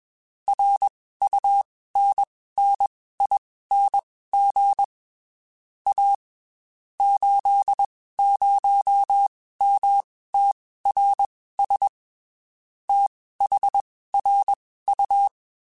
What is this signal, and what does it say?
Morse code, heard as a keyed tone, about 21 wpm, spaced as for 15 wpm, 800 Hz. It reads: RUNNING A 80MTRS THRU
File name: QSOs / QSOs-mp3s / WD0FBT 1@20